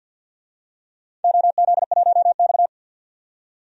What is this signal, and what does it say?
Morse code, heard as a keyed tone, about 51 wpm, 700 Hz. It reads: OZ1X
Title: DXCC-100 Call Signs - 4 Characters - 50wpm